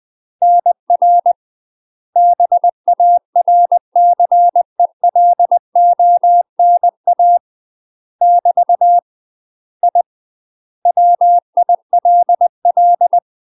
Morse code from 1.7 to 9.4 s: BARCELONA <BT>